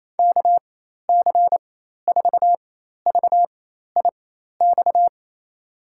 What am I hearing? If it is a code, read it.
Morse code, heard as a keyed tone, about 28 wpm, spaced as for 13 wpm, 700 Hz. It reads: KC4VIX
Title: USA Call Signs - 6 Characters - 4x Character Spacing - 28wpm